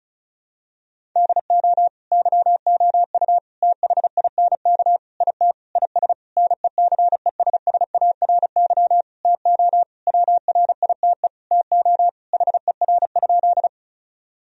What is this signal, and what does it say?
Morse code, heard as a keyed tone, about 34 wpm, 700 Hz. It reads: DO YOU THINK IT IS NECESSARY TO WRITE TO HER?
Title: Sentences from Top 1000 Words - Review Part 1 - 35wpm